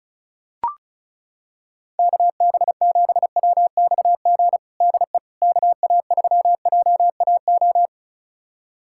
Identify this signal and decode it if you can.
Morse code, heard as a keyed tone, about 36 wpm, 700 Hz. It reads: KB7WXG DE KA3JAO
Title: USA Callsign DE Callsign 35wpm